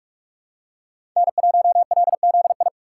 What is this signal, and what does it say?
Morse code, heard as a keyed tone, about 45 wpm, 700 Hz. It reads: N1LZI
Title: USA Call Signs - 4x Character Spacing - 45wpm